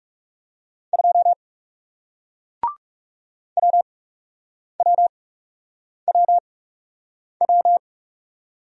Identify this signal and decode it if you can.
Morse code, heard as a keyed tone, about 40 wpm, spaced as for 8 wpm, 700 Hz. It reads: 2 WWWW